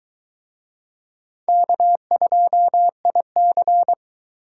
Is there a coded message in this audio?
Yes: Morse code from 1.5 to 3.9 s, heard as a keyed tone, about 22 wpm, 700 Hz. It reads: K2IC